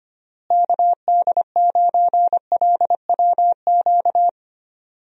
Morse code, heard as a keyed tone, about 25 wpm, 700 Hz. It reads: KD9LWQ
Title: USA Call Signs - 6 Characters - Repeated 3x - 25wpm